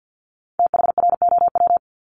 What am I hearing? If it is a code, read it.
Morse code, heard as a keyed tone, about 50 wpm, 690 Hz. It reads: THROW
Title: Multi-letter Phonemes Repeated 3x 50wpm